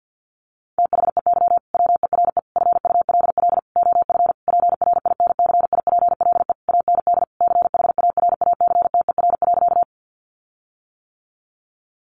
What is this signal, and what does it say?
Morse code, heard as a keyed tone, about 50 wpm, 700 Hz. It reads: THEY WERE FULL OF PRINCIPLE AND CHARACTER.
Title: Sentences from Top 1400 Words 50wpm